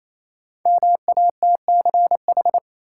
Morse code, heard as a keyed tone, about 28 wpm, 700 Hz. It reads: MATCH
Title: Sentences from Top 1600 Words - New Word Drill - 28wpm